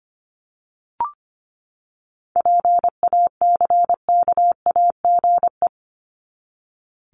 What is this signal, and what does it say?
Morse code, heard as a keyed tone, about 25 wpm, 700 Hz. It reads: PACKAGE
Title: Top 1501-1750 Words 25wpm